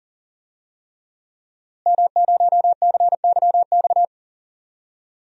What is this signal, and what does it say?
Morse code, heard as a keyed tone, about 40 wpm, 700 Hz. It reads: M0CYX
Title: DXCC-100 Call Signs - 5 Characters - Repeated 3x - 40wpm